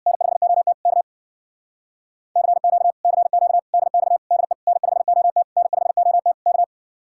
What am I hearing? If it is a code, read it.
Morse code, heard as a keyed tone, about 64 wpm, 700 Hz. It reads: N5OT K CQ CQ DX DE N5OT N5OT K